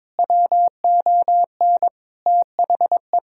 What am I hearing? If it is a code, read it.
Morse code, heard as a keyed tone, about 23 wpm, 700 Hz. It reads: WON THE
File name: practice / 22wpm / Sentences with 8-Word Phrases - 22wpm